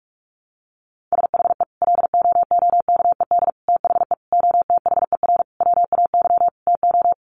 Morse code, heard as a keyed tone, about 44 wpm, 700 Hz. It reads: SHE LOOKED THE OTHER WAY TO
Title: Sentences from Top 800 Words 45wpm